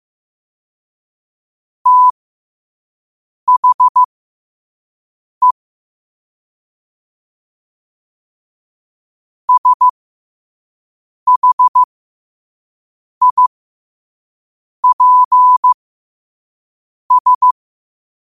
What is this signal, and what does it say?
Morse code, heard as a keyed tone, about 15 wpm, spaced as for 5 wpm, 1000 Hz. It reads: THE SHIPS
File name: qotdfiles / 5wpm